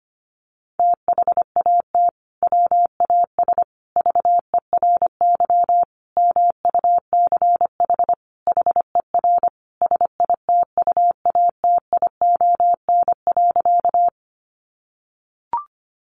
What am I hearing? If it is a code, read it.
Morse code, heard as a keyed tone, about 25 wpm, 700 Hz. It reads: THAT WAS VERY MUCH HER SITUATION.
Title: Sentences from Top 1200 Words 25wpm